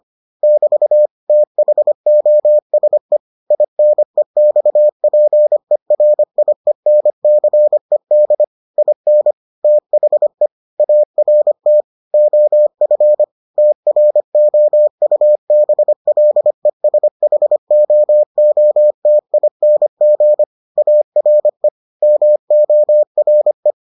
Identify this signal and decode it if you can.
Morse code, heard as a keyed tone, about 25 wpm, 600 Hz. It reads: <BT> THOSE INEXPERIENCED IN THE ART OF TROUBLESHOOTING ARE MORE